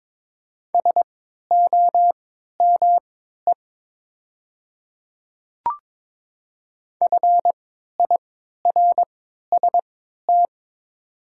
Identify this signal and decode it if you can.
Morse code, heard as a keyed tone, about 22 wpm, spaced as for 12 wpm, 700 Hz. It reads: SOME FIRST
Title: 3x Character Spacing - Top 100 Words - 22wpm